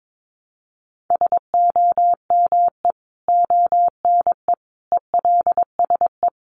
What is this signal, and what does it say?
Morse code, heard as a keyed tone, about 22 wpm, 700 Hz. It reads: SOME ONE ELSE
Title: Sentences from Top 1100 Words 22wpm